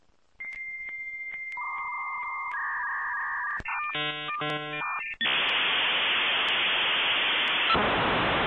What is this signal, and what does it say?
Dial-up modem handshake.